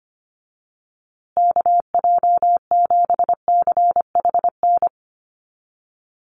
Morse code, heard as a keyed tone, about 25 wpm, 700 Hz. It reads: KJ7CHN
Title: USA Call Signs Repeated 3x 25wpm